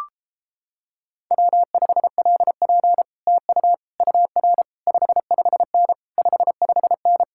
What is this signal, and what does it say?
Morse code, heard as a keyed tone, about 29 wpm, 700 Hz. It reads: W5LP TU UR 55N 55N